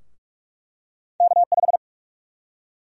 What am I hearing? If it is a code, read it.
Morse code, heard as a keyed tone, about 46 wpm, 700 Hz. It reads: K5
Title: USA Call Sign Prefixes - Repeated 2x - 45wpm